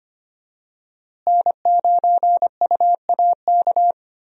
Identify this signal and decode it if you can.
Morse code, heard as a keyed tone, about 25 wpm, 700 Hz. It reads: N9UAK
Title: USA Call Signs - 5 Characters - Repeated 2x - 25wpm